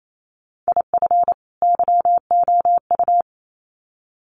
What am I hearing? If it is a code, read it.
Morse code, heard as a keyed tone, about 28 wpm, 700 Hz. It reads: IF YOU
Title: Sets of 2 Words 28wpm